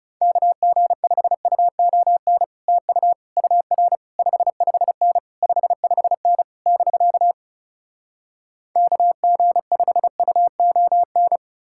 Morse code, heard as a keyed tone, about 37 wpm, 700 Hz. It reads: KG5UON TU UR 55N 55N <BK> KG5UON
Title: Speed-Racing Common QSO Phrases Encore - 30wpm